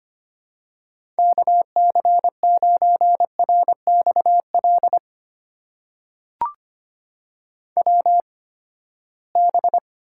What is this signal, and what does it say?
Morse code, heard as a keyed tone, about 25 wpm, spaced as for 17 wpm, 700 Hz. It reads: KC9RXL WB
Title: USA Call Signs - 6 Characters - 8x Character Spacing - 25wpm